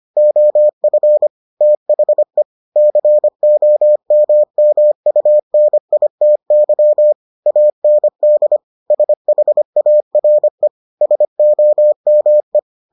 Morse code, heard as a keyed tone, about 25 wpm, 600 Hz. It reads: OF THE COMMUNITY AND SHARE SOME